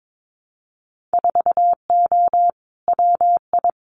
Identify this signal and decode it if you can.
Morse code, heard as a keyed tone, about 22 wpm, 700 Hz. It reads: 4O WI
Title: Winter Field Day Exchanges 22wpm